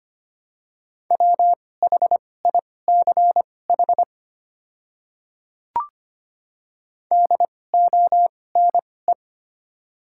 Morse code, heard as a keyed tone, about 25 wpm, spaced as for 18 wpm, 700 Hz. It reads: WHICH DONE
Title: 2x Character Spacing - Top 500 Words - 25wpm